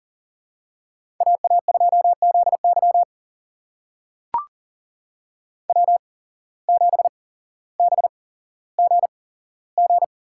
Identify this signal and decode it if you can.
Morse code, heard as a keyed tone, about 40 wpm, spaced as for 17 wpm, 700 Hz. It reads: AA2ZY W7BGG
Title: USA Call Signs - 5 Characters - 8x Character Spacing - 40wpm